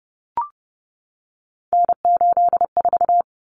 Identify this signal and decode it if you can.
Morse code, heard as a keyed tone, about 30 wpm, 700 Hz. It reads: N84
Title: Common Cut Numbers - 3-Digit 30wpm